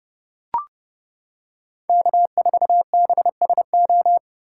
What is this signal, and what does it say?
Morse code, heard as a keyed tone, about 30 wpm, 700 Hz. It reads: K4BSO